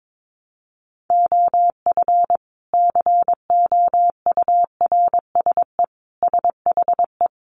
Morse code, heard as a keyed tone, about 22 wpm, 700 Hz. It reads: OF COURSE SHE
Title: Sentences from Top 500 Words 22wpm